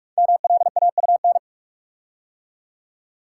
Morse code, heard as a keyed tone, about 44 wpm, 700 Hz. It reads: MLAUN
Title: Code Groups - 5 Letters - 45wpm